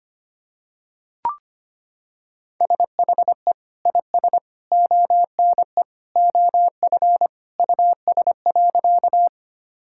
Morse code, heard as a keyed tone, about 25 wpm, 700 Hz. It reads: SHE IS ONE OF US.
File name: practice / 25wpm / Sentences with 3-Word Phrases - 25wpm